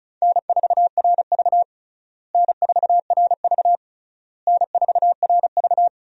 Morse code, heard as a keyed tone, about 36 wpm, 700 Hz. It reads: N4RV N4RV N4RV
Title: USA Call Signs - 4 Characters - Repeated 3x - 35wpm